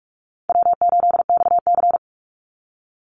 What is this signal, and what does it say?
Morse code, heard as a keyed tone, about 44 wpm, 700 Hz. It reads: W8XC
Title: Easy Call Signs - 45wpm